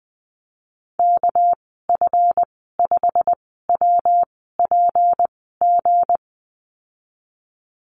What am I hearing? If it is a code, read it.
Morse code, heard as a keyed tone, about 20 wpm, spaced as for 15 wpm, 700 Hz. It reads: KF5WPG
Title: 207 Lesson 33g - Letter j - Call Signs 20wpm